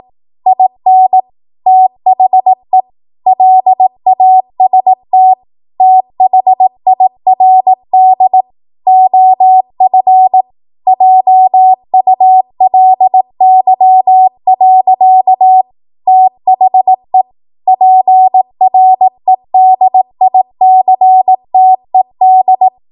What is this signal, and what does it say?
Morse code, heard as a keyed tone, about 18 wpm, 750 Hz. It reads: IN THE LAST THIRD OF JULY. THE PREDICTED
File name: Morse / 18WPM